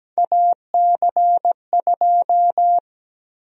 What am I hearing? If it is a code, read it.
Morse code, heard as a keyed tone, about 17 wpm, 700 Hz. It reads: AC2
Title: USA Call Sign Prefixes - Repeated 2x - 17wpm